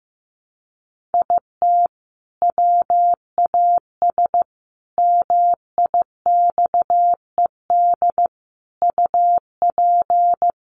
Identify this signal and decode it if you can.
Morse code, heard as a keyed tone, about 15 wpm, 700 Hz. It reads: IT WAS MIXED UP
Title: Sentences from Top 1500 Words 15wpm